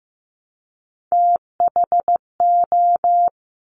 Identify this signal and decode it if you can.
Morse code, heard as a keyed tone, about 15 wpm, 700 Hz. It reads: THO
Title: Top Three Letter Combinations in Words 15wpm